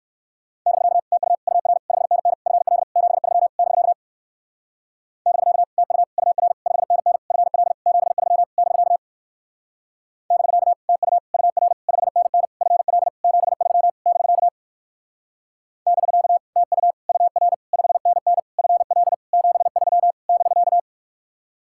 Morse code, heard as a keyed tone, about 58 wpm, 700 Hz. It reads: <BK> TU UR 5NN FL 73 <BK> <BK> TU UR 5NN FL 73 <BK> <BK> TU UR 5NN FL 73 <BK> <BK> TU UR 5NN FL 73 <BK>